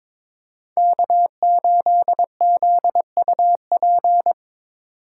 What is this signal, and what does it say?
Morse code, heard as a keyed tone, about 22 wpm, 700 Hz. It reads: K8ZUP
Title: Speed Racing - USA Call Signs 22wpm